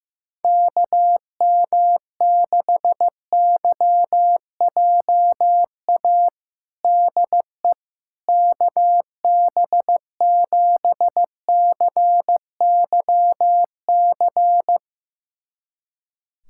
Morse code, heard as a keyed tone, about 15 wpm, 700 Hz. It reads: KM6YJA DE KB7CYC
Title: USA Callsign DE Callsign - 6 Characters - 15wpm